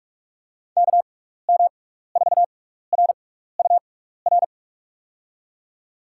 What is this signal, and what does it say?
Morse code, heard as a keyed tone, about 45 wpm, 700 Hz. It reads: K M 4 R U R
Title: USA Call Signs - 6 Characters - 6x Character Spacing - 45wpm